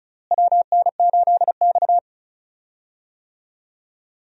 Morse code, heard as a keyed tone, about 36 wpm, 700 Hz. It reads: WN8X
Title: USA Call Signs - 4 Characters - Repeated 3x - 35wpm